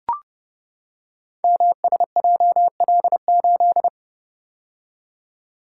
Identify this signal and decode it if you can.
Morse code, heard as a keyed tone, about 30 wpm, 700 Hz. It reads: MSJL8
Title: Code Groups - 5 Letter:Numbers - 30wpm